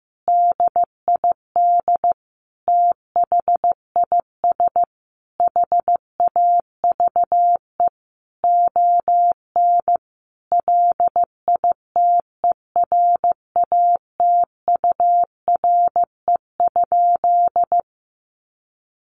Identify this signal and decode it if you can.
Morse code, heard as a keyed tone, about 15 wpm, 700 Hz. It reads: DID THIS HAVE ON LITERATURE?
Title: Sentences from Top 1000 Words 15wpm